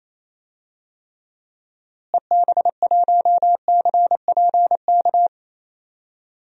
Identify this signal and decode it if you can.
Morse code, heard as a keyed tone, about 28 wpm, 700 Hz. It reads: EB1CPK
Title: DXCC-100 Call Signs - 6 Characters - Repeated 3x - 28wpm